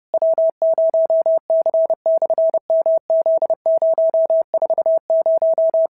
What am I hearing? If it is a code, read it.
Morse code, heard as a keyed tone, about 30 wpm, 650 Hz. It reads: W0C/MZ040
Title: SOTA QSO Practice 30wpm